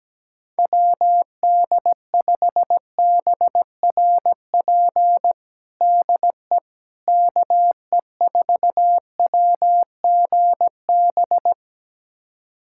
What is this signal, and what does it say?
Morse code, heard as a keyed tone, about 17 wpm, 700 Hz. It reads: WD5BRP DE KE4WGB
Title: USA Callsign DE Callsign 17wpm